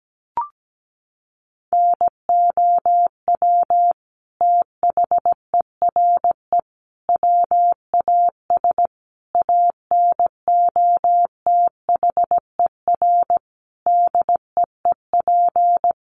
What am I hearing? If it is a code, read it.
Morse code, heard as a keyed tone, about 17 wpm, 700 Hz. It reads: NOW THERE WAS ANOTHER DEEP
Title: Sentences from Top 1300 Words 17wpm